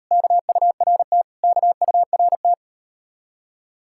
Morse code, heard as a keyed tone, about 39 wpm, 700 Hz. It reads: KURT KURT